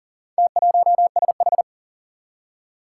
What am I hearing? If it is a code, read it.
Morse code, heard as a keyed tone, about 40 wpm, 700 Hz. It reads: T1SH
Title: Four Challenging Letter-Number Combinations 40wpm